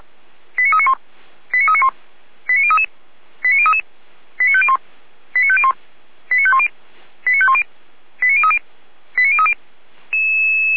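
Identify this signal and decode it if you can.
Five-tone selective call (ZVEI), heard as quick runs of five tones, tones of about 70 ms. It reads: 89391 89391 89033 89033 89501 89501 89510 89510 89030 89030
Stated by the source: ZVEI five-tone call 89145 received by SDR.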